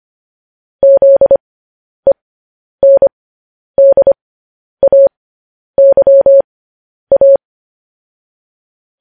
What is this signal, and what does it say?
Morse code, heard as a keyed tone, about 25 wpm, spaced as for 10 wpm, 550 Hz. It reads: ZENDAYA